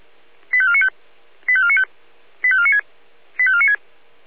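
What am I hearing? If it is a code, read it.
Five-tone selective call (ZVEI), heard as quick runs of five tones, tones of about 70 ms. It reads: 85495 85495 85496 85496
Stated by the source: ZVEI five-tone selective call received by SDR.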